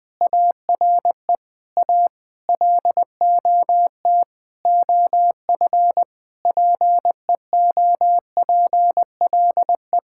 Morse code, heard as a keyed tone, about 20 wpm, 700 Hz. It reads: ARE A LOT OF PEOPLE
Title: Sentences with 9-Word Phrases - 20wpm